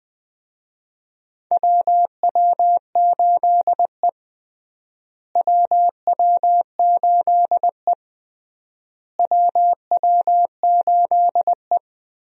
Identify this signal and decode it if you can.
Morse code, heard as a keyed tone, about 20 wpm, 700 Hz. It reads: WW8E WW8E WW8E
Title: USA Call Signs - 4 Characters - Repeated 3x - 20wpm